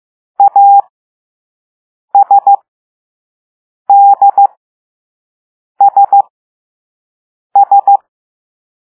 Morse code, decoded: ASDSS